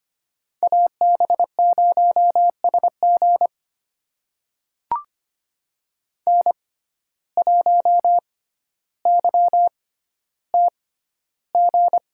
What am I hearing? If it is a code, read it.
Morse code, heard as a keyed tone, about 25 wpm, spaced as for 13 wpm, 700 Hz. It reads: AB0SG N1YTG
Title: USA Call Signs - 5 Characters - 6x Character Spacing - 25wpm